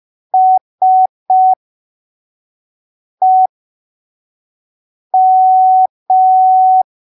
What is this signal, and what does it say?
Morse code, heard as a keyed tone, about 5 wpm, spaced as for 3 wpm, 750 Hz. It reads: SEM
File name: arquivos / texto / texto 21 05ppm